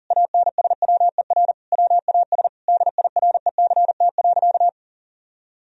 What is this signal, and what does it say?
Morse code, heard as a keyed tone, about 40 wpm, 700 Hz. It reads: ANSWER WAS DIRECT.
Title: Sentences from Top 1000 Words - Review Part 2 - 40wpm